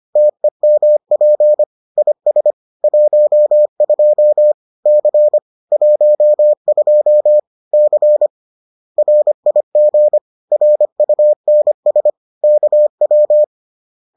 Morse code, heard as a keyed tone, about 25 wpm, 600 Hz. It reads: TEMP IS 12 C 12 C RIG RUNS KW